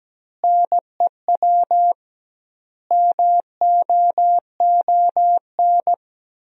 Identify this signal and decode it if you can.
Morse code, heard as a keyed tone, about 17 wpm, 700 Hz. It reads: NEW MOON
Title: 2x Word-Spacing - Sentences from Top 500 Words - 17wpm